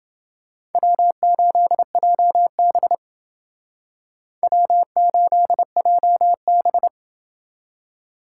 Morse code, heard as a keyed tone, about 29 wpm, 700 Hz. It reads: W8JB W8JB